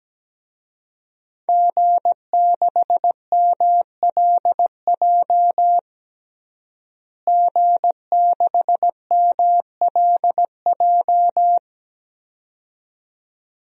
Morse code, heard as a keyed tone, about 17 wpm, 700 Hz. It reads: G6MLJ G6MLJ
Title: DXCC-100 Call Signs - 5 Characters - Repeated 2x - 17wpm